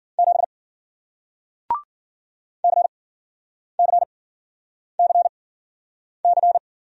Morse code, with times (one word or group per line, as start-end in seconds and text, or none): 0.2-0.4 7
2.6-6.6 CCCC